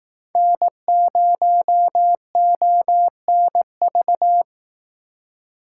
Morse code, heard as a keyed tone, about 19 wpm, 700 Hz. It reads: N0ONV